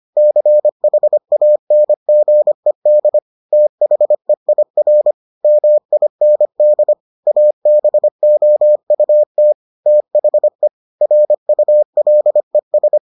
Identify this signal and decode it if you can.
Morse code, heard as a keyed tone, about 25 wpm, 600 Hz. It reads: CHANGED THEIR MIND ABOUT THE RULES